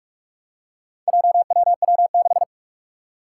Morse code, heard as a keyed tone, about 46 wpm, 700 Hz. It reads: JWW6
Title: Four Challenging Letter-Number Combinations 45wpm